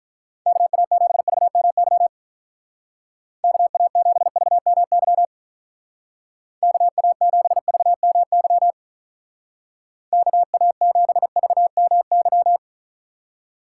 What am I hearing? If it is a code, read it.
Morse code, heard as a keyed tone, about 44 wpm, 700 Hz. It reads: KA7VMY KA7VMY KA7VMY KA7VMY